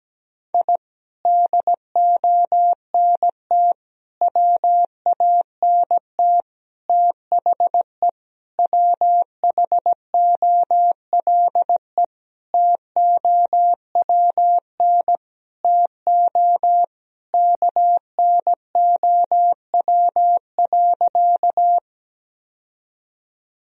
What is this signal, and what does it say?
Morse code, heard as a keyed tone, about 17 wpm, 700 Hz. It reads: I DONT WANT THE WHOLE TOWN TO KNOW.
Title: Sentences from Top 1000 Words - Review Part 2 - 17wpm